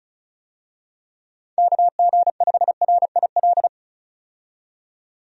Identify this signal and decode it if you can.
Morse code, heard as a keyed tone, about 36 wpm, 700 Hz. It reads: KG5RIL
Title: USA Call Signs - 6 Characters - 35wpm